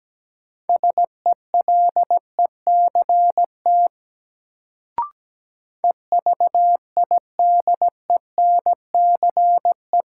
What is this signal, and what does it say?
Morse code, heard as a keyed tone, about 17 wpm, 700 Hz. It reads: SELECT EVIDENCE
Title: Sentences from Top 1300 Words - New Word Drill - 17wpm